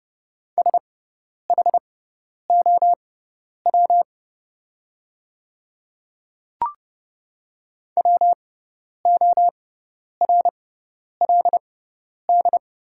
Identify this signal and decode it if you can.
Morse code, heard as a keyed tone, about 30 wpm, spaced as for 10 wpm, 700 Hz. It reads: SHOW WORLD
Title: 6x Character Spacing - Top 200 Words - 30wpm